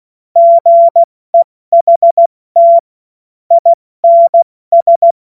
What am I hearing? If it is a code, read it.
Morse code, heard as a keyed tone, about 16 wpm, spaced as for 14 wpm, 680 Hz. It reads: GEHT INS
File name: Morsekurs / Witze / Witz 24. August 2021, 70 F80